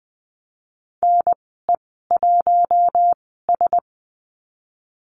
Morse code, heard as a keyed tone, about 20 wpm, 700 Hz. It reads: N E 1 S
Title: 076 Lesson 13g - Letter L - Call Signs 20wpm